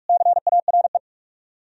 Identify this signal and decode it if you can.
Morse code, heard as a keyed tone, about 58 wpm, 700 Hz. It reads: CARE